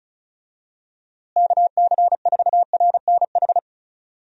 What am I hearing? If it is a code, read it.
Morse code, heard as a keyed tone, about 36 wpm, 700 Hz. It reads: KC4RNH
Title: USA Call Signs - 6 Characters - Repeated 3x - 35wpm